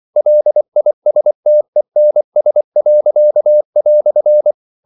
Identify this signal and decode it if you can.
Morse code, heard as a keyed tone, about 24 wpm, 600 Hz. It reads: LISTENS."
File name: CW / Practice / EnglishQuotations0224